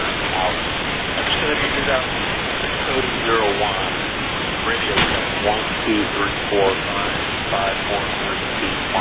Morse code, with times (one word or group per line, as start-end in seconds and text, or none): none